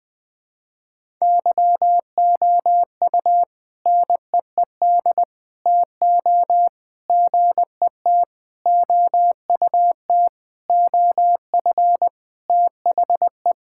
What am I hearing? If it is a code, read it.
Morse code, heard as a keyed tone, about 20 wpm, 700 Hz. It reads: YOU NEED TO GET OUT OF THE
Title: Sentences with 4-Word Phrases - 20wpm